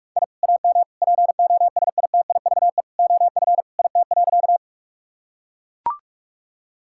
Morse code, heard as a keyed tone, about 42 wpm, 700 Hz. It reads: I AM POSITIVE OF IT.
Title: Sentences from Top 1000 Words - Review Part 5 - 45wpm